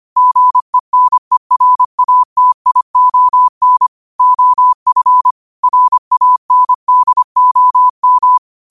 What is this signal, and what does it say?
Morse code, heard as a keyed tone, about 25 wpm, 1000 Hz. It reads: GENERATION OF RANDOM
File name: qotdfiles / 25wpm